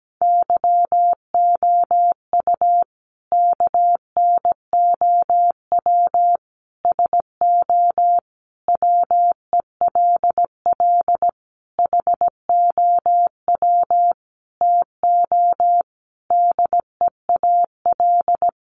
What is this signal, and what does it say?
Morse code, heard as a keyed tone, about 17 wpm, 700 Hz. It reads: YOU KNOW SO WELL HOW TO DEAL
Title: Sentences from Top 800 Words 17wpm